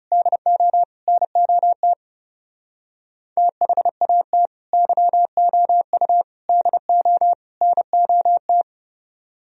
Morse code, heard as a keyed tone, about 34 wpm, 700 Hz. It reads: DO NOT THAT YOU DO NOT
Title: Speed-Racing - Sets of 4 Words 30wpm